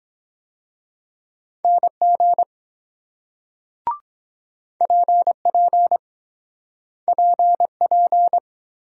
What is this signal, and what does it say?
Morse code, heard as a keyed tone, about 23 wpm, 700 Hz. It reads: NG PP PP